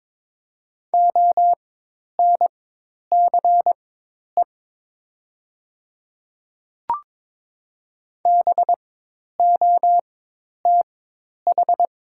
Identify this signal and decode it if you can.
Morse code, heard as a keyed tone, about 22 wpm, spaced as for 10 wpm, 700 Hz. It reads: ONCE BOTH
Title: 4x Character Spacing - Top 300 Words - 22wpm